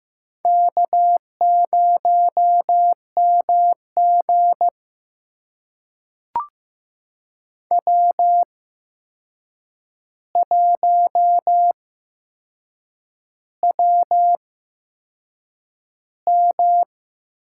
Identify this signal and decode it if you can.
Morse code, heard as a keyed tone, about 15 wpm, spaced as for 6 wpm, 700 Hz. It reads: K0MG W1WM
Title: USA Call Signs - 4 Characters - 8x Character Spacing - 15wpm